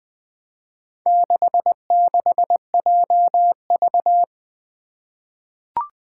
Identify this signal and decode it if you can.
Morse code, heard as a keyed tone, about 20 wpm, 700 Hz. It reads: 66JV
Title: Four Challenging Letter-Number Combinations 20wpm